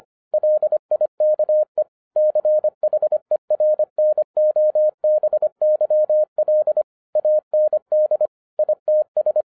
Morse code, heard as a keyed tone, about 25 wpm, 610 Hz. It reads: LIKE CHERNOBYL AND ITS
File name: 2011-07-mp3 / n39 THE 3.11 2 25wpm